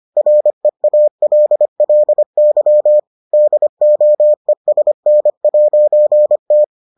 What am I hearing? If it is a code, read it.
Morse code, heard as a keyed tone, about 25 wpm, 600 Hz. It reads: REALLY DOESN'T